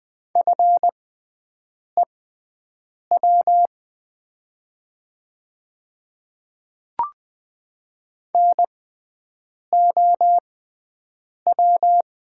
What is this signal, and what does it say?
Morse code, heard as a keyed tone, about 20 wpm, spaced as for 7 wpm, 700 Hz. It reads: FEW NOW
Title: 6x Character Spacing - Top 300 Words - 20wpm